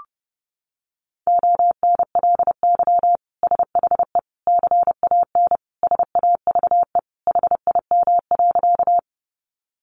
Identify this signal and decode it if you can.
Morse code, heard as a keyed tone, about 30 wpm, 700 Hz. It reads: ONLY SHE CAN SAVE HIM.
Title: Sentences from Top 1100 Words 30wpm